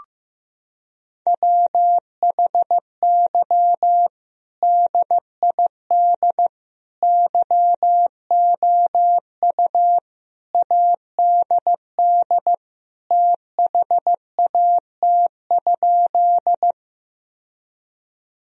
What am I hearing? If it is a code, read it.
Morse code, heard as a keyed tone, about 15 wpm, 700 Hz. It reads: WHY DID YOU ADD THAT?
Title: Sentences from Top 1000 Words - Review Part 1 - 15wpm